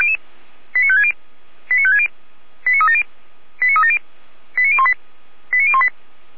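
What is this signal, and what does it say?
Five-tone selective call (ZVEI), heard as quick runs of five tones, tones of about 70 ms. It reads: 89560 89560 89370 89370 89018 89018
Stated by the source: Five-tone ZVEI selective call received on SDR, from the audio archive of Feuerwehr Orlamünde.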